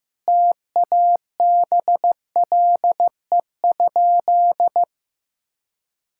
Morse code, heard as a keyed tone, about 15 wpm, 700 Hz. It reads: TABLE?
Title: Sentences from Top 1000 Words - Review Part 6 - 15wpm